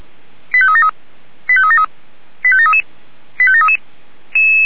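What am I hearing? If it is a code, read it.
Five-tone selective call (ZVEI), heard as quick runs of five tones, tones of about 70 ms. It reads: 85373 85373 85733 85733
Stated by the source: ZVEI five-tone selective call received by SDR.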